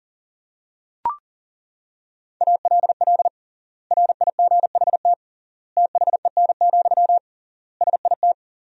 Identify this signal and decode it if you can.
Morse code, heard as a keyed tone, about 40 wpm, 700 Hz. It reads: ALL RIGHT THEN, SIT
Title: 3x Word-Spacing - Sentences from Top 500 Words - 40wpm